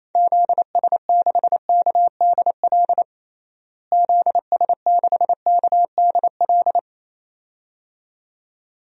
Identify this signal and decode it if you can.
Morse code, heard as a keyed tone, about 28 wpm, 700 Hz. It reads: ZS6KDL ZS6KDL
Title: DXCC-100 Call Signs - 6 Characters - Repeated 2x - 28wpm